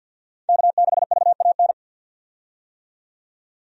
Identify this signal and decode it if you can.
Morse code, heard as a keyed tone, about 49 wpm, 700 Hz. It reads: K6VAN
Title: USA Call Signs - 5 Characters - 50wpm